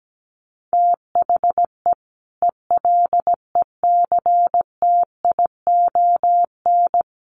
Morse code, heard as a keyed tone, about 17 wpm, 700 Hz. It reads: THE ELECTION
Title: Sentences from Top 1500 Words 17wpm